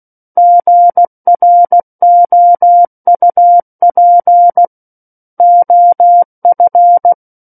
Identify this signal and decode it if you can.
Morse code, heard as a keyed tone, about 16 wpm, 700 Hz. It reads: GROUP OF
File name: Book One / 07-Chapter 7 - 16 WPM